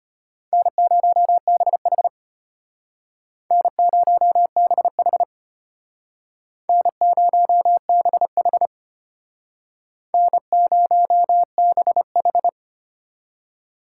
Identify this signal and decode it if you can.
Morse code, heard as a keyed tone, about 32 wpm, 700 Hz. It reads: N0BH N0BH N0BH N0BH